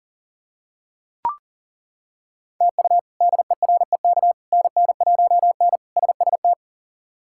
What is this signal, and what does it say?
Morse code, heard as a keyed tone, about 38 wpm, 700 Hz. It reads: TU DEREK NN1N SST
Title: K1USN SST QSO Practice - 40wpm